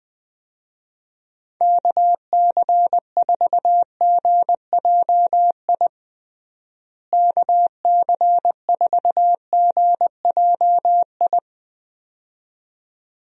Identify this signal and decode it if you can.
Morse code, heard as a keyed tone, about 20 wpm, 700 Hz. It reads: KC4GJI KC4GJI
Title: USA Call Signs - 6 Characters - Repeated 2x - 20wpm